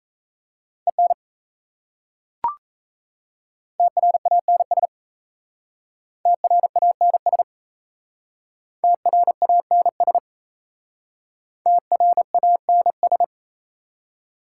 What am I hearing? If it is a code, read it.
Morse code, heard as a keyed tone, about 37 wpm, 700 Hz. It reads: EN TRANS TRANS TRANS TRANS